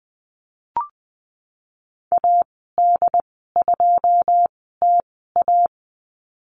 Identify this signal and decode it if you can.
Morse code, heard as a keyed tone, about 20 wpm, spaced as for 15 wpm, 700 Hz. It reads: AD2TA
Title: 147 Lesson 24g - Letter y - Call Signs 20wpm